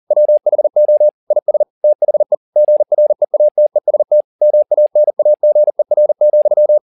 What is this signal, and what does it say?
Morse code, heard as a keyed tone, about 40 wpm, 600 Hz. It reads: WHO IS THE GREATEST MANAGER,